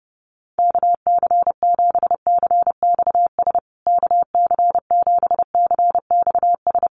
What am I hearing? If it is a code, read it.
Morse code, heard as a keyed tone, about 30 wpm, 700 Hz. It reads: KC7CXS KC7CXS
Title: USA Call Signs Repeated 3x 30wpm